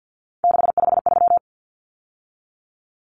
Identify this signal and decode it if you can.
Morse code, heard as a keyed tone, about 50 wpm, 700 Hz. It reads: 653
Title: Numbers - 3-Digit 50wpm